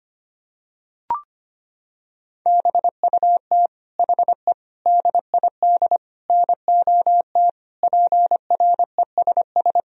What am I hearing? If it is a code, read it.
Morse code, heard as a keyed tone, about 25 wpm, 700 Hz. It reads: BUT HE DID NOT PRESS
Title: Sentences from Top 1000 Words - Review Part 1 - 25wpm